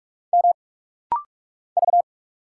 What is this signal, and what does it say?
Morse code, heard as a keyed tone, about 46 wpm, 690 Hz. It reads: M V
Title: Single Letters - Rapid-Fire 45wpm